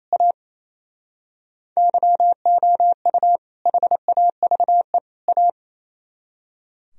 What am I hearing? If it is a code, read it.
Morse code, heard as a keyed tone, about 35 wpm, 700 Hz. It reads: A YOU HAVE A